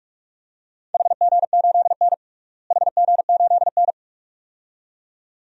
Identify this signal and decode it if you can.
Morse code, heard as a keyed tone, about 46 wpm, 700 Hz. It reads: HG8N HG8N
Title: DXCC-100 Call Signs - 4 Characters - Repeated 2x - 45wpm